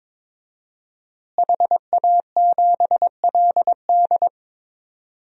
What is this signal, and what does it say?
Morse code, heard as a keyed tone, about 22 wpm, 700 Hz. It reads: HA7LD
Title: DXCC-100 Call Signs - 5 Characters - Repeated 3x - 22wpm